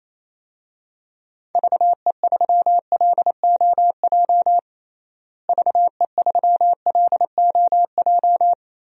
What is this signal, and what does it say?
Morse code, heard as a keyed tone, about 28 wpm, 700 Hz. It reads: VE3LOJ VE3LOJ
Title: DXCC-100 Call Signs - 6 Characters - Repeated 3x - 28wpm